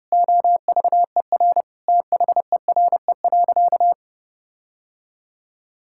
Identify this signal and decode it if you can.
Morse code, heard as a keyed tone, about 30 wpm, 700 Hz. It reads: OVER THERE.
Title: Sentences from Top 1600 Words 30wpm